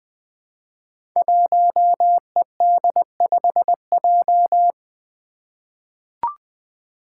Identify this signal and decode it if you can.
Morse code, heard as a keyed tone, about 20 wpm, 700 Hz. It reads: 1ED5J
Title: Five Challenging Letter-Number Combinations 20wpm